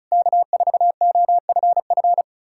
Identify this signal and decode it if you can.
Morse code, heard as a keyed tone, about 37 wpm, 700 Hz. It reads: K4OFF
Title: Speed-Racing Common QSO Phrases Encore - 30wpm